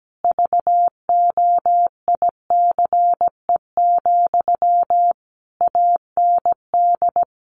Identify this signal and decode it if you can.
Morse code, heard as a keyed tone, about 17 wpm, 700 Hz. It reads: VOICE, AND
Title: Sentences from Top 1100 Words 17wpm